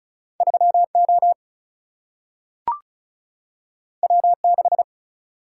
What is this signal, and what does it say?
Morse code, heard as a keyed tone, about 36 wpm, 700 Hz. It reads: 3O W6
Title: Single-Letter Single-Number 35wpm